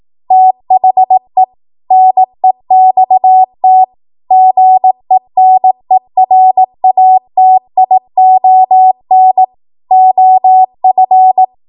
Morse code, heard as a keyed tone, about 18 wpm, 750 Hz. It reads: THE NEXT GENERATION OF
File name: Morse / 18WPM